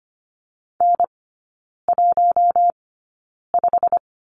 Morse code, heard as a keyed tone, about 25 wpm, spaced as for 9 wpm, 700 Hz. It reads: N15
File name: course / 25:10wpm / 093 Lesson 16e - Letter c - Three Characters 25wpm